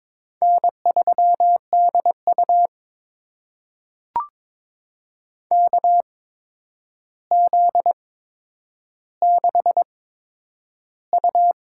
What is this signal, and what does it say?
Morse code, heard as a keyed tone, about 22 wpm, spaced as for 9 wpm, 700 Hz. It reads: N3DU KZ6U
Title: USA Call Signs - 4 Characters - 8x Character Spacing - 22wpm